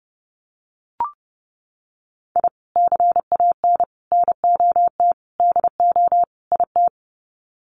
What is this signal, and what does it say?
Morse code, heard as a keyed tone, about 30 wpm, 700 Hz. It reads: I CAN NOT DO IT
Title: Sets of 5 Words Encore 30wpm